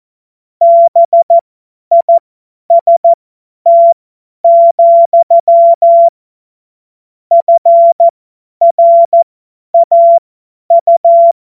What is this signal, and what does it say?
Morse code, heard as a keyed tone, about 14 wpm, spaced as for 10 wpm, 680 Hz. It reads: BIST, FRAU
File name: Morsekurs / Witze / Witz 13. September 2022 50 F70